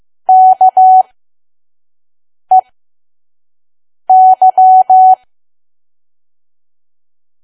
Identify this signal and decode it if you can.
Morse code, heard as a keyed tone, about 15 wpm, spaced as for 5 wpm, 750 Hz. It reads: KEY
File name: Archive / 5 WPM files / PM